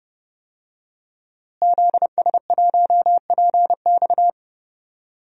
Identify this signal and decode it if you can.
Morse code, heard as a keyed tone, about 31 wpm, 700 Hz. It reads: ZS1PX
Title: DXCC-100 Call Signs - 5 Characters - Repeated 2x - 30wpm